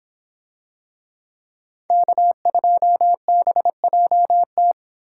Morse code, heard as a keyed tone, about 26 wpm, 700 Hz. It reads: K2BJT